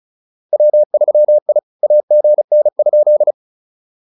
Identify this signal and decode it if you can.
Morse code, heard as a keyed tone, about 36 wpm, 600 Hz. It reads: W3I AGN?